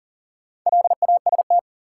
Morse code, heard as a keyed tone, about 40 wpm, 700 Hz. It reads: LAST
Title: Sets of 2 Words - New Word Drill - 40wpm